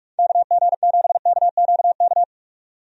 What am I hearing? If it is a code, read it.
Morse code, heard as a keyed tone, about 46 wpm, 700 Hz. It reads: KG7KQK
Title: USA Call Signs - 6 Characters - 45wpm